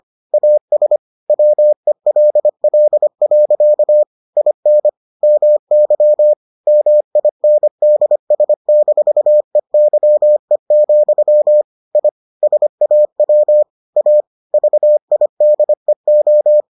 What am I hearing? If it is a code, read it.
Morse code, heard as a keyed tone, about 25 wpm, 600 Hz. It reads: AS WELL. IN MY MINDS-EYE, I SAW A VIDEO